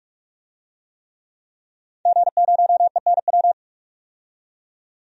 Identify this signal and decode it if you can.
Morse code, heard as a keyed tone, about 46 wpm, 700 Hz. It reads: G0ENW
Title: DXCC-100 Call Signs - 5 Characters - 45wpm